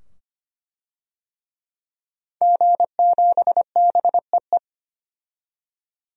Morse code, heard as a keyed tone, about 25 wpm, 700 Hz. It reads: G7BEE
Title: DXCC-100 Call Signs - 5 Characters - 25wpm